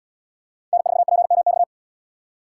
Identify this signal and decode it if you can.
Morse code, heard as a keyed tone, about 72 wpm, 700 Hz. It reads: N4PMX